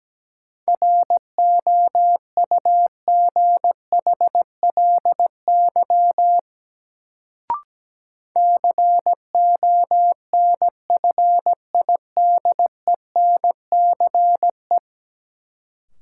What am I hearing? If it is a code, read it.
Morse code, heard as a keyed tone, about 17 wpm, 700 Hz. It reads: ROUGHLY CONFIDENCE
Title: Sentences from Top 1500 Words - New Word Drill Alt - 17wpm